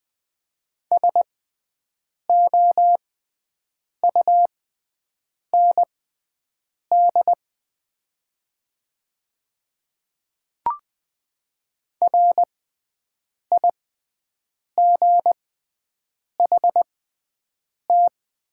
Morse code, heard as a keyed tone, about 20 wpm, spaced as for 7 wpm, 700 Hz. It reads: SOUND RIGHT
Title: 6x Character Spacing - Top 200 Words - 20wpm